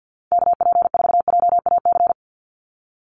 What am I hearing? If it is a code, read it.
Morse code, heard as a keyed tone, about 50 wpm, 700 Hz. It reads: KP4JAC
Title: USA Call Signs 50wpm